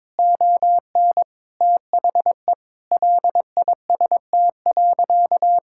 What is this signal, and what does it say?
Morse code, heard as a keyed tone, about 22 wpm, 700 Hz. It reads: ON THE LIST.
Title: Sentences from Top 1000 Words - Review Part 6 - 22wpm